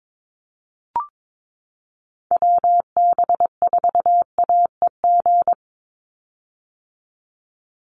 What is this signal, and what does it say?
Morse code, heard as a keyed tone, about 22 wpm, 700 Hz. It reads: WB4AEG
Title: Call Signs - 22wpm